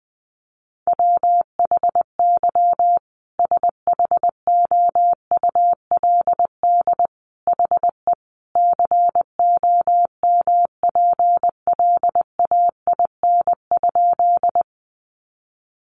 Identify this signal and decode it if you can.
Morse code, heard as a keyed tone, about 19 wpm, 700 Hz. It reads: WHY SHOULD HE COMPLAIN?
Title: Sentences from Top 1300 Words 20wpm